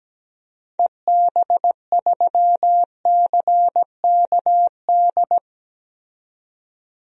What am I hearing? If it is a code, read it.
Morse code, heard as a keyed tone, about 17 wpm, 700 Hz. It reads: EB3CKD
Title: DXCC-100 Call Signs - 6 Characters - 17wpm